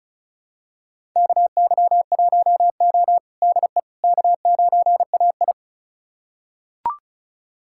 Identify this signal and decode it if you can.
Morse code, heard as a keyed tone, about 36 wpm, 700 Hz. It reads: KY1O DE K9AI
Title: USA Callsign DE Callsign - 4 Characters - 35wpm